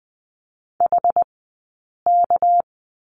Morse code, heard as a keyed tone, about 20 wpm, spaced as for 8 wpm, 700 Hz. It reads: HK
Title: 197 Lesson 32d - Letter k - Two Characters 20wpm